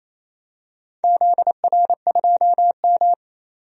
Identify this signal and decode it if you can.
Morse code, heard as a keyed tone, about 28 wpm, 700 Hz. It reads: ZR2M
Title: DXCC-100 Call Signs - 4 Characters - 28wpm